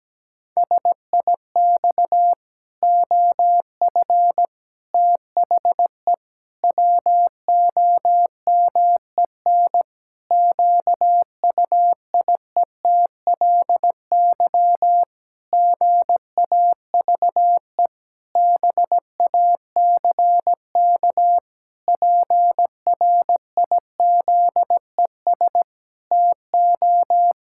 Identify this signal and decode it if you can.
Morse code, heard as a keyed tone, about 17 wpm, 700 Hz. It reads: SIX OF THE WOMEN QUIETLY GAVE BACK PRIZES TO
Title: Pangram Challenge v2 - 17wpm